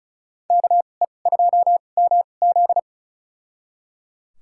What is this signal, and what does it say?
Morse code, heard as a keyed tone, about 34 wpm, spaced as for 25 wpm, 700 Hz. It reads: KE2MZ
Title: USA Call Signs - 5 Characters - 2x Character Spacing - 35wpm